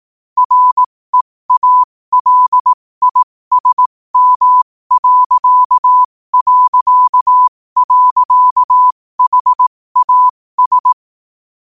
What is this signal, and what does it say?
Morse code, heard as a keyed tone, about 18 wpm, spaced as for 15 wpm, 1000 Hz. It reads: REALISM...HAS
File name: qotdfiles / 15wpm